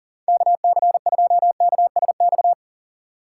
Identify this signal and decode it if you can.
Morse code, heard as a keyed tone, about 40 wpm, 700 Hz. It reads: KC2KSX
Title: USA Call Signs - 6 Characters - 2x Character Spacing - 40wpm